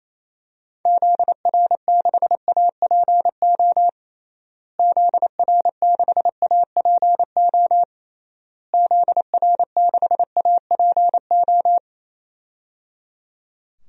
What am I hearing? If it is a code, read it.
Morse code, heard as a keyed tone, about 28 wpm, 700 Hz. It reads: ZR6APO ZR6APO ZR6APO
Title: DXCC-100 Call Signs - 6 Characters - Repeated 3x - 28wpm